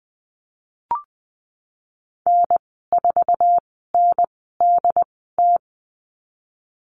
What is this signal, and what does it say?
Morse code, heard as a keyed tone, about 20 wpm, spaced as for 15 wpm, 700 Hz. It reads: N4NDT
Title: 076 Lesson 13g - Letter L - Call Signs 20wpm